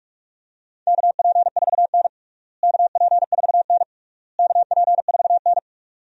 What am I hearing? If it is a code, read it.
Morse code, heard as a keyed tone, about 47 wpm, 700 Hz. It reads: KP4N KP4N KP4N
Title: DXCC-100 Call Signs - 4 Characters - Repeated 3x - 45wpm